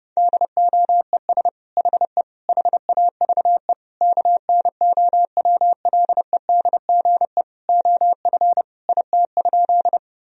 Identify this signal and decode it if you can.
Morse code, heard as a keyed tone, about 30 wpm, 700 Hz. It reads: DOES HE HAVE KNOWLEDGE OF IT?
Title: Sentences from Top 1000 Words - Review Part 3 - 30wpm